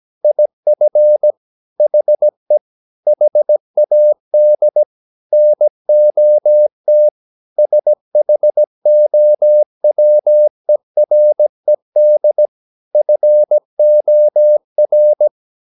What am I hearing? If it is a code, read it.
Morse code, heard as a keyed tone, about 17 wpm, 600 Hz. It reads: IF HE HAD NOT SHOWERED FOR
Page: Romanen Never af Ken Follet på engelsk.